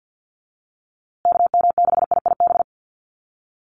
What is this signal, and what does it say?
Morse code, heard as a keyed tone, about 50 wpm, 700 Hz. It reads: KM6IIB
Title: USA Call Signs Repeated 2x 50wpm